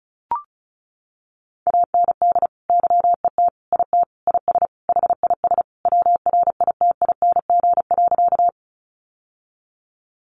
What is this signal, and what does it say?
Morse code, heard as a keyed tone, about 34 wpm, 700 Hz. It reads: AND YET IT IS HIS WRITING.
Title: Sentences from Top 800 Words 35wpm